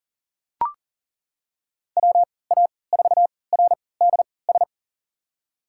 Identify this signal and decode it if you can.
Morse code, heard as a keyed tone, about 40 wpm, spaced as for 23 wpm, 700 Hz. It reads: WA4RDS
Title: USA Call Signs - 6 Characters - 3x Character Spacing - 40wpm